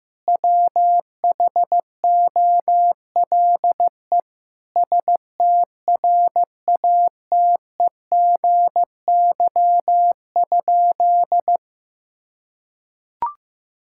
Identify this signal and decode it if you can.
Morse code, heard as a keyed tone, about 15 wpm, 700 Hz. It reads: WHOLE STRATEGY?
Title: Sentences from Top 1000 Words - Review Part 5 - 15wpm